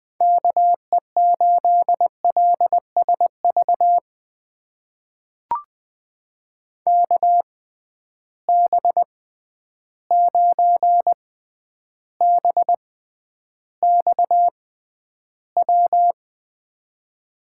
Morse code, heard as a keyed tone, about 20 wpm, spaced as for 10 wpm, 700 Hz. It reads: KE8LSV KB9BXW
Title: USA Call Signs - 6 Characters - 6x Character Spacing - 20wpm